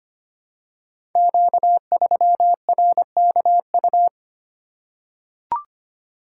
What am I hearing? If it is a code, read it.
Morse code, heard as a keyed tone, about 25 wpm, 700 Hz. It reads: Q3RKU
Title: Code Groups - 5 Letter:Numbers - 25wpm